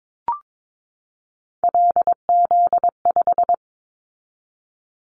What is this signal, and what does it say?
Morse code, heard as a keyed tone, about 22 wpm, 700 Hz. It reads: LZ5
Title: Callsign Prefixes for Top 100 DXCC - 22wpm